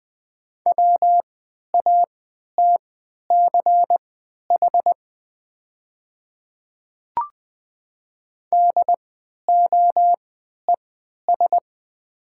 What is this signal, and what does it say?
Morse code, heard as a keyed tone, about 20 wpm, spaced as for 11 wpm, 700 Hz. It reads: WATCH DOES
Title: 3x Character Spacing - Top 500 Words - 20wpm